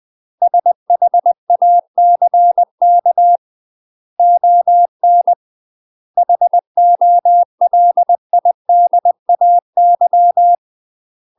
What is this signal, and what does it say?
Morse code, heard as a keyed tone, about 20 wpm, 700 Hz. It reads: SHACK ON HOLIDAY
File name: UK MASTER QSOs TRAINING / 20WPM / UK MASTER QSOs 0052